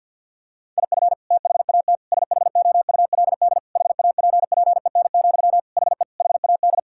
Morse code, heard as a keyed tone, about 49 wpm, 700 Hz. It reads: IF THAT SHOULD HAPPEN, HE HAD
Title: Sentences from Top 1000 Words - Review Part 6 - 50wpm